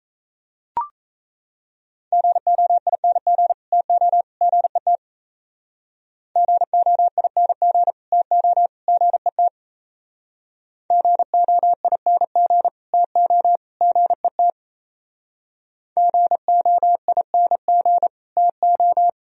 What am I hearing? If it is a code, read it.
Morse code, heard as a keyed tone, about 35 wpm, 700 Hz. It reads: GOING TO GET GOING TO GET GOING TO GET GOING TO
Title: Speed-Racing - Sets of 3 Words 28wpm